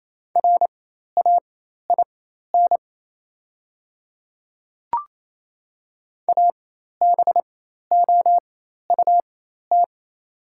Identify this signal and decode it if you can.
Morse code, heard as a keyed tone, about 28 wpm, spaced as for 13 wpm, 700 Hz. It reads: RAIN ABOUT
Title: 4x Character Spacing - Top 300 Words - 28wpm